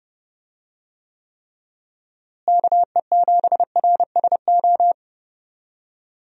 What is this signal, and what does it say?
Morse code, heard as a keyed tone, about 30 wpm, 700 Hz. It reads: KE7RSO
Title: USA Call Signs - 6 Characters - Repeated 3x - 30wpm